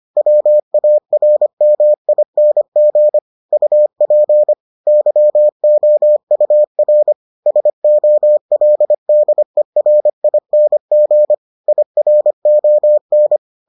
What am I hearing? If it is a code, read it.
Morse code, heard as a keyed tone, about 25 wpm, 600 Hz. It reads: WARMING UP YOUR SOLDERING IRON